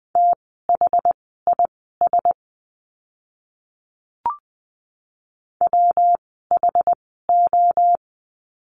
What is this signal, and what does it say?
Morse code, heard as a keyed tone, about 20 wpm, spaced as for 14 wpm, 700 Hz. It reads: THIS WHO
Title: 237 Lesson 38f - Letter q - Words 20wpm